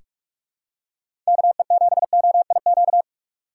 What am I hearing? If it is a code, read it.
Morse code, heard as a keyed tone, about 42 wpm, 700 Hz. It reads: KE7OIQ
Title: USA Call Signs - 6 Characters - 4x Character Spacing - 45wpm